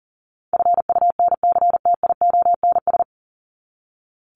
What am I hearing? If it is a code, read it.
Morse code, heard as a keyed tone, about 40 wpm, 700 Hz. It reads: FUNCTIONS
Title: ES:S Suffix - Top 100 Words with Base Words 40wpm